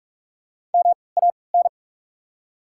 Morse code, heard as a keyed tone, about 42 wpm, 690 Hz. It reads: M A N
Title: 3x Character Spacing - Top 200 Words - 45wpm